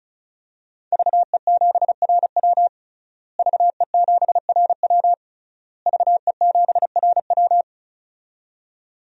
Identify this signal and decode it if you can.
Morse code, heard as a keyed tone, about 34 wpm, 700 Hz. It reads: VE7RW VE7RW VE7RW
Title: DXCC-100 Call Signs - 5 Characters - Repeated 3x - 35wpm